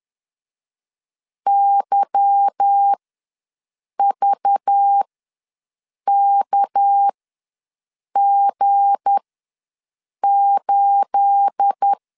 Morse code, heard as a keyed tone, about 10 wpm, spaced as for 6 wpm, 800 Hz. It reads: YVKG8
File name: LEZIONI / 9 28 ws 10wpm 4124